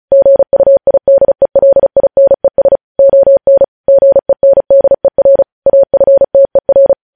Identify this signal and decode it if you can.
Morse code, heard as a keyed tone, about 36 wpm, 550 Hz. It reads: GUIDELINES ON GENDER AFTER
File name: podcast / News headlines-1x-35x35